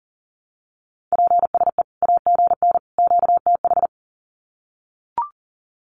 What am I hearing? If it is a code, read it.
Morse code, heard as a keyed tone, about 40 wpm, 700 Hz. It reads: PSE AGN QTH
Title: Common QSO Phrases 40wpm